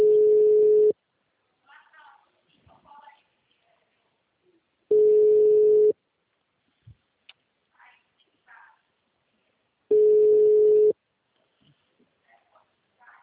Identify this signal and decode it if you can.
Ringback tone (Europe, 425 Hz), with speech.
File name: out- -2019- - - .136